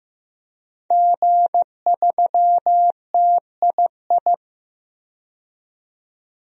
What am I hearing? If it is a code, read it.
Morse code, heard as a keyed tone, about 15 wpm, 700 Hz. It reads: G3TII
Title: DXCC-100 Call Signs - 5 Characters - 15wpm